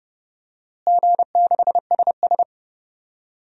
Morse code, heard as a keyed tone, about 30 wpm, 700 Hz. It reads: G6SS
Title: DXCC-100 Call Signs - 4 Characters - Repeated 3x - 30wpm